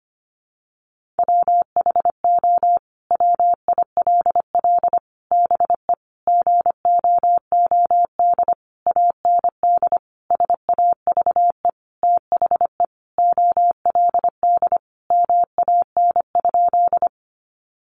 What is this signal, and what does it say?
Morse code, heard as a keyed tone, about 25 wpm, 700 Hz. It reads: WHO WILL BE GOOD AND SAVE THE OLD MAN?
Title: Sentences from Top 1100 Words 25wpm